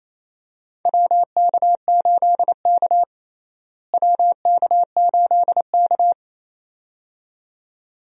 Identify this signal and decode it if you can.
Morse code, heard as a keyed tone, about 28 wpm, 700 Hz. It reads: WK8K WK8K
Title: USA Call Signs - 4 Characters - Repeated 2x - 28wpm